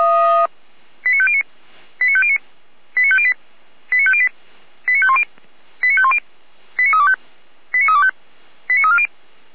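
Five-tone selective call (ZVEI), heard as quick runs of five tones, tones of about 70 ms. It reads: 89509 89509 89508 89508 89510 89510 89325 89325 89340
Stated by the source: ZVEI five-tone call received by SDR, from the feuerwehr-orlamuende.org fire brigade audio server, station f85360.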